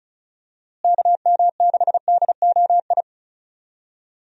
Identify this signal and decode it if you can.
Morse code, heard as a keyed tone, about 34 wpm, 700 Hz. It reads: KM6DOI
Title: USA Call Signs - 6 Characters - Repeated 3x - 35wpm